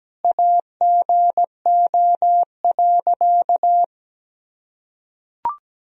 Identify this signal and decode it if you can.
Morse code, heard as a keyed tone, about 17 wpm, 700 Hz. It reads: AGO.
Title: Sentences from Top 1000 Words - Review Part 3 - 17wpm